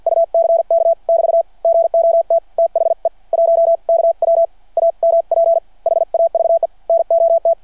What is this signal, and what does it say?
Morse code, heard as a keyed tone, about 53 wpm, spaced as for 45 wpm, 670 Hz. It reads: UCK <BT> GOT THE 1KW AMP HAVE NOT
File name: archived / CRSnet